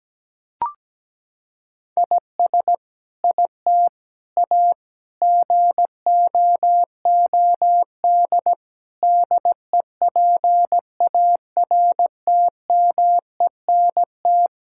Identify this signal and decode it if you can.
Morse code, heard as a keyed tone, about 17 wpm, 700 Hz. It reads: IS IT A GOOD DEPARTMENT
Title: Sentences from Top 1000 Words - Review Part 5 - 17wpm